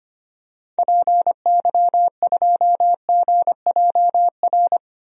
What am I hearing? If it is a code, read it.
Morse code, heard as a keyed tone, about 25 wpm, 700 Hz. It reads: PY2GJR
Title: DXCC-100 Call Signs - 6 Characters - 25wpm